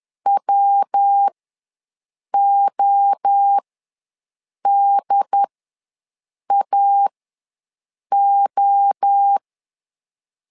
Morse code, heard as a keyed tone, about 10 wpm, spaced as for 6 wpm, 800 Hz. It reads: WODAO
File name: LEZIONI / 3 MWRO / 3 MWRO ws 10wpm 2310